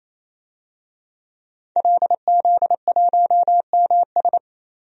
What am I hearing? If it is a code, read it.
Morse code, heard as a keyed tone, about 28 wpm, 700 Hz. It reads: LZ1MS
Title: DXCC-100 Call Signs - 5 Characters - 28wpm